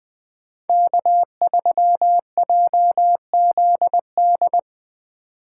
Morse code, heard as a keyed tone, about 20 wpm, 700 Hz. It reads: K3JZD
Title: 2023 SOTA Chaser Call Signs - 20wpm